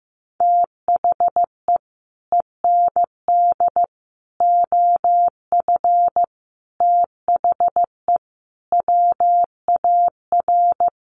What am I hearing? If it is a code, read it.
Morse code, heard as a keyed tone, about 15 wpm, 700 Hz. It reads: THE END OF THE WAR
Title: Sets of 5 Words 15wpm